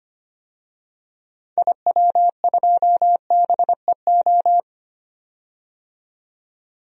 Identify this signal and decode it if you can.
Morse code, heard as a keyed tone, about 25 wpm, 700 Hz. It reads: IW2BEO
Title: DXCC-100 Call Signs - 6 Characters - 25wpm